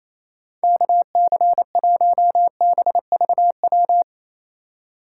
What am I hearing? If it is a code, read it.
Morse code, heard as a keyed tone, about 28 wpm, 700 Hz. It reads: KC1BVW
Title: USA Call Signs - 6 Characters - 2x Character Spacing - 28wpm